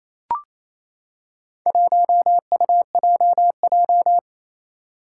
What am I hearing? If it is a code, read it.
Morse code, heard as a keyed tone, about 28 wpm, 700 Hz. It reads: 1UJJ
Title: Four Challenging Letter-Number Combinations 28wpm